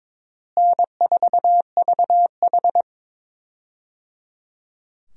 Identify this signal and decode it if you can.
Morse code, heard as a keyed tone, about 22 wpm, 700 Hz. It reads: N4VH
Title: USA Call Signs - 4 Characters - Repeated 3x - 22wpm